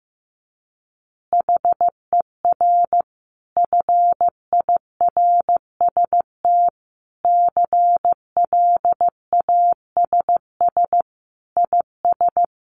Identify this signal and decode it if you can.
Morse code, heard as a keyed tone, about 15 wpm, 700 Hz. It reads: HER FIRST CLASS IS
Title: Additional Easy Phrases 15wpm